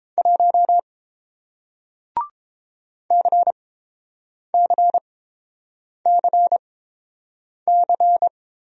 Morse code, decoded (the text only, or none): 1 CCCC